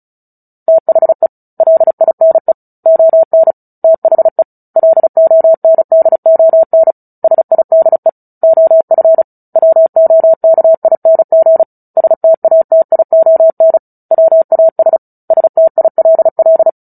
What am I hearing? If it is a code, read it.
Morse code, heard as a keyed tone, about 35 wpm, 650 Hz. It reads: THE LINE ON THE LONDON SIDE OF WOKING STATION WAS STILL
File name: Book Two / 09-Chapter 9 - 35 WPM